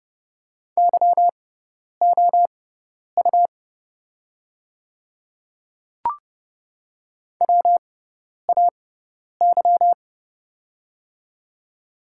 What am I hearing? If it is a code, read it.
Morse code, heard as a keyed tone, about 30 wpm, spaced as for 10 wpm, 700 Hz. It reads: YOU WAY
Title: 6x Character Spacing - Top 200 Words - 30wpm